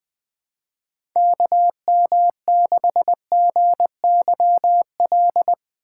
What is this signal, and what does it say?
Morse code, heard as a keyed tone, about 20 wpm, 700 Hz. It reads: KM6GYL
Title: USA Call Signs - 6x Character Spacing - 20wpm